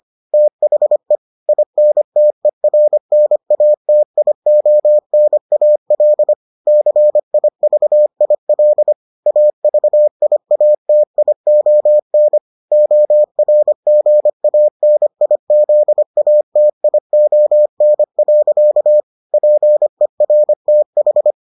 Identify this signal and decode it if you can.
Morse code, heard as a keyed tone, about 25 wpm, 600 Hz. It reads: THE INTERNATIONAL CIVIL AVIATION ORGANIZATION. PERTH